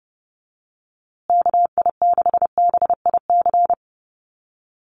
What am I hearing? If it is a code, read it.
Morse code, heard as a keyed tone, about 30 wpm, 700 Hz. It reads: KI6BIC
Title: USA Call Signs Repeated 3x 30wpm